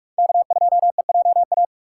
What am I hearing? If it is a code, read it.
Morse code, heard as a keyed tone, about 43 wpm, 700 Hz. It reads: K2EJA
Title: USA Call Signs - 5 Characters - 4x Character Spacing - 45wpm